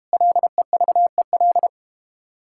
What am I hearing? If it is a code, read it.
Morse code, heard as a keyed tone, about 32 wpm, 700 Hz. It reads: LEVEL